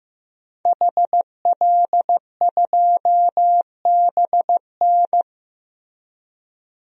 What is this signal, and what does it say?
Morse code, heard as a keyed tone, about 16 wpm, 700 Hz. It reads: HL2BN
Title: DXCC-100 Call Signs - 5 Characters - Repeated 2x - 15wpm